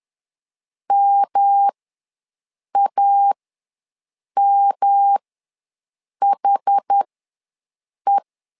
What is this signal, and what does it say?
Morse code, heard as a keyed tone, about 10 wpm, spaced as for 6 wpm, 800 Hz. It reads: MAMHE
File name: LEZIONI / 3 MWRO / 3 M ws 10wpm 2121